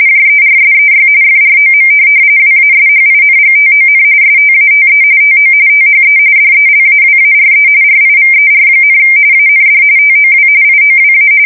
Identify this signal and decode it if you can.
RTTY, about 45.45 baud. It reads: JERSEY FROM WASHINGTON STATE TO THE DELAWARE RIVER, AND TOW IOWA
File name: fsk170 / USS-New-Jersey